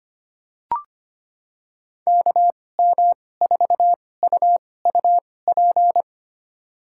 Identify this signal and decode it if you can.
Morse code, heard as a keyed tone, about 25 wpm, spaced as for 18 wpm, 700 Hz. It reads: KM4UUP
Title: USA Call Signs - 6 Characters - 2x Character Spacing - 25wpm